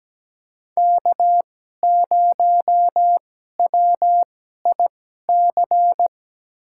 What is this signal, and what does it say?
Morse code, heard as a keyed tone, about 17 wpm, spaced as for 12 wpm, 700 Hz. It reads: K0WIC
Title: USA Call Signs - 5 Characters - 2x Character Spacing - 17wpm